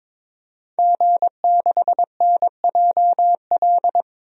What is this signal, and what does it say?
Morse code, heard as a keyed tone, about 22 wpm, 700 Hz. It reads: G6NJL
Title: DXCC-100 Call Signs - 5 Characters - 22wpm